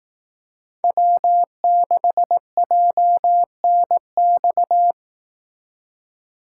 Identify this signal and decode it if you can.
Morse code, heard as a keyed tone, about 18 wpm, 700 Hz. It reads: W6JNX
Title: Speed Racing - USA Call Signs 15wpm